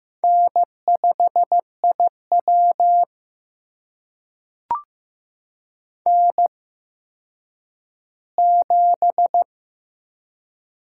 Morse code, heard as a keyed tone, about 15 wpm, spaced as for 9 wpm, 700 Hz. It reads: N5IW N7
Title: USA Call Signs - 4 Characters - 8x Character Spacing - 15wpm